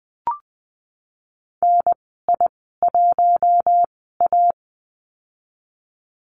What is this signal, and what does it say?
Morse code, heard as a keyed tone, about 20 wpm, spaced as for 14 wpm, 700 Hz. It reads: NI1A
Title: 069 Lesson 12g - Letter d - Call Signs 20wpm